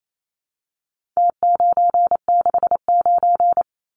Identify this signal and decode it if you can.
Morse code, heard as a keyed tone, about 28 wpm, 700 Hz. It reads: T969
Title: Numbers - Contest Style - 4-Digit 28wpm